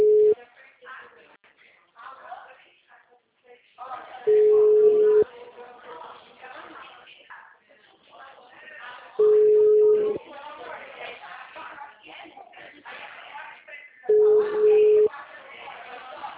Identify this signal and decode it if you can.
Ringback tone (Europe, 425 Hz), with speech.
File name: out- -2010- - - .35443